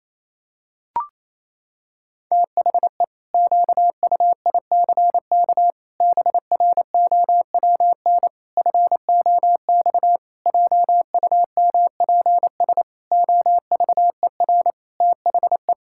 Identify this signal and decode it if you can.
Morse code, heard as a keyed tone, about 28 wpm, 700 Hz. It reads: THE QUICK BROWN FOX JUMPS OVER THE
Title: Pangram Challenge v2 - 28wpm